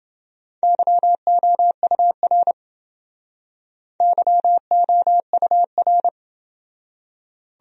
Morse code, heard as a keyed tone, about 29 wpm, 700 Hz. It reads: YOUR YOUR